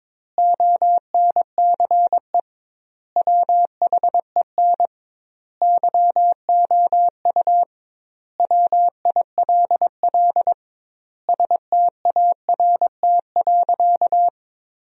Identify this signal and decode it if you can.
Morse code, heard as a keyed tone, about 22 wpm, 700 Hz. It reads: ONCE WHEN YOU WILL START.
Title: 2x Word-Spacing - Sentences from Top 300 Words - 22wpm